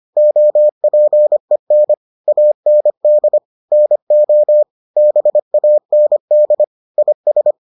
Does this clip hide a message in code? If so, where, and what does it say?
Yes: Morse code from 0.2 to 7.5 s, heard as a keyed tone, about 25 wpm, 600 Hz. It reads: OPEN AND NO BAND IS